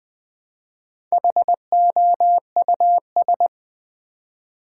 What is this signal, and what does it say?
Morse code, heard as a keyed tone, about 20 wpm, 700 Hz. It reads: HOUS
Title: Build A Word - 1 Letter At A Time - 5-Letter Words - 20wpm